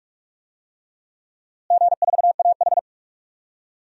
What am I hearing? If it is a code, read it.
Morse code, heard as a keyed tone, about 42 wpm, 700 Hz. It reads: G4AH
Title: DXCC-100 Call Signs - 4 Characters - Repeated 3x - 45wpm